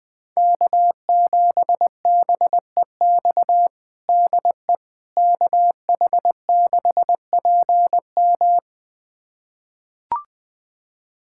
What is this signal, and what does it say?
Morse code, heard as a keyed tone, about 20 wpm, 700 Hz. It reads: K7BEX DE KH6PM
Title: USA Callsign DE Callsign - 5 Characters - 20wpm